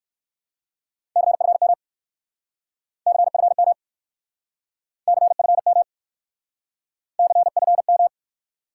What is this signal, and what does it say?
Morse code, heard as a keyed tone, about 66 wpm, 700 Hz. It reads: CFM CFM CFM CFM